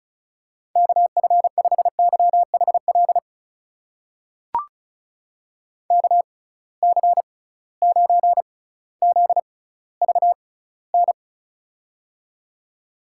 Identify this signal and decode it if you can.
Morse code, heard as a keyed tone, about 36 wpm, spaced as for 18 wpm, 700 Hz. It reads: KF5YHL KC9ZVN
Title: USA Call Signs - 6 Characters - 6x Character Spacing - 35wpm